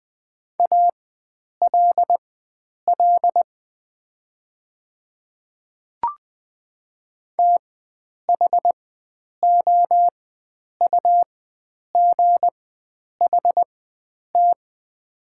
Morse code, heard as a keyed tone, about 20 wpm, spaced as for 9 wpm, 700 Hz. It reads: ALL THOUGHT
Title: 4x Character Spacing - Top 300 Words - 20wpm